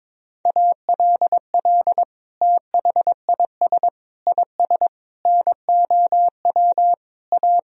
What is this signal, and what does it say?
Morse code, heard as a keyed tone, about 22 wpm, 700 Hz. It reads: ALL THIS IS NOW A
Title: Sentences from Top 1000 Words - Review Part 3 - 22wpm